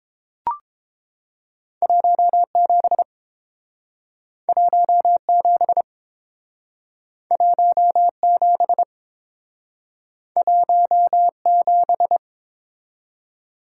Morse code, heard as a keyed tone, about 29 wpm, 700 Hz. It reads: 17 17 17 17